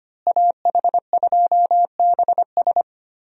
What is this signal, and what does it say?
Morse code, heard as a keyed tone, about 25 wpm, 700 Hz. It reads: AH2BS
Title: USA Call Signs - 5 Characters - Repeated 3x - 25wpm